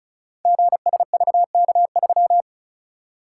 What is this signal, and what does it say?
Morse code, heard as a keyed tone, about 36 wpm, 700 Hz. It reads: GSVK3
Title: Code Groups - 5 Letter:Numbers - 35wpm